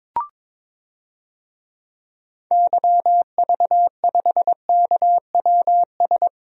Morse code, heard as a keyed tone, about 22 wpm, 700 Hz. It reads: YV5KWS
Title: DXCC-100 Call Signs - 6 Characters - 22wpm